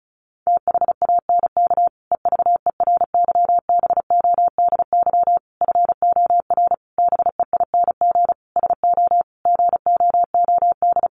Morse code, heard as a keyed tone, about 34 wpm, 700 Hz. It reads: THANK EVERYBODY FOR BEING SO GOOD
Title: Sentences from Top 1400 Words 35wpm